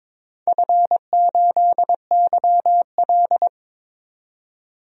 Morse code, heard as a keyed tone, about 22 wpm, 700 Hz. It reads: F8YL
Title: DXCC-100 Call Signs - 4 Characters - 22wpm